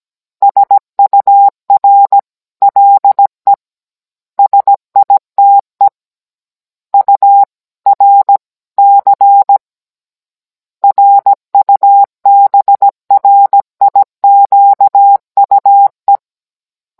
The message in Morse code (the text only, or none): SUR LE SITE URC RUBRIQUE